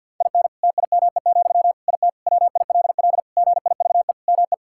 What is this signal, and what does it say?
Morse code, heard as a keyed tone, about 50 wpm, 700 Hz. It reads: IN TIME, IT WILL GIVE ME
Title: Sentences from Top 1600 Words 50wpm